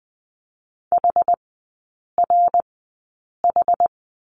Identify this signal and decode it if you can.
Morse code, heard as a keyed tone, about 20 wpm, spaced as for 8 wpm, 690 Hz. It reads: HRH
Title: 081 Lesson 14e - Number 2 - Three Characters 20wpm